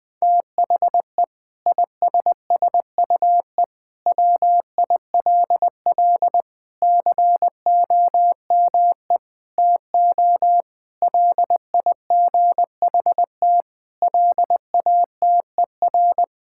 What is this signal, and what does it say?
Morse code, heard as a keyed tone, about 20 wpm, 700 Hz. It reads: THE ISSUE WILL COME TO LIGHT LATER
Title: Sentences from Top 1000 Words - Review Part 6 - 20wpm